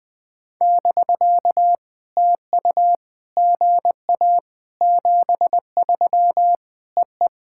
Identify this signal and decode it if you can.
Morse code, heard as a keyed tone, about 20 wpm, 700 Hz. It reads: <BK> TU GA 73 EE
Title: POTA QSO Practice - 20wpm